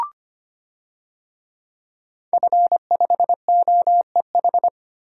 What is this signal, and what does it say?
Morse code, heard as a keyed tone, about 25 wpm, 700 Hz. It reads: F5OEH
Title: DXCC-100 Call Signs - 5 Characters - 25wpm